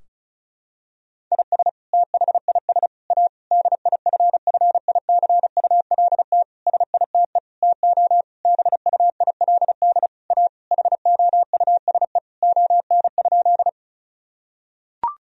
Morse code, heard as a keyed tone, about 36 wpm, 700 Hz. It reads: IS THIS A DIFFICULT SITE TO BUILD A HOUSE ON?
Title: Sentences from Top 1000 Words - Review Part 4 - 35wpm